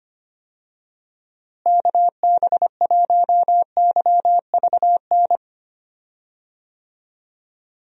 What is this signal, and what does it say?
Morse code, heard as a keyed tone, about 25 wpm, 700 Hz. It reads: KB1YVN
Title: USA Call Signs - 6 Characters - Repeated 2x - 25wpm